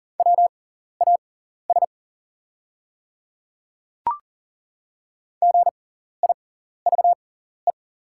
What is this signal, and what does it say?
Morse code, heard as a keyed tone, about 40 wpm, spaced as for 14 wpm, 690 Hz. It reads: WAS GIVE
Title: 6x Character Spacing - Top 500 Words - 40wpm